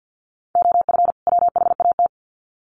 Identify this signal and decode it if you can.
Morse code, heard as a keyed tone, about 50 wpm, 700 Hz. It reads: OF WHAT
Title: Sets of 2 Words 50wpm